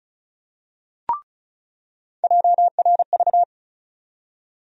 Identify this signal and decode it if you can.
Morse code, heard as a keyed tone, about 36 wpm, 700 Hz. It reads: JRV
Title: USA Call Sign Postfixes - 3 Letters 35wpm